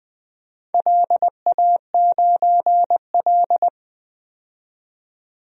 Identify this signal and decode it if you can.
Morse code, heard as a keyed tone, about 20 wpm, 700 Hz. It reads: LA9L
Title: DXCC-100 Call Signs - 4 Characters - 20wpm